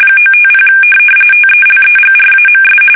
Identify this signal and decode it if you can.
RTTY, about 45.45 baud. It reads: 950 TELEGRAMS